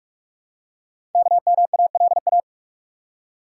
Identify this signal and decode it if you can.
Morse code, heard as a keyed tone, about 44 wpm, 700 Hz. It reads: KMALA
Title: Code Groups - 5 Letters - 45wpm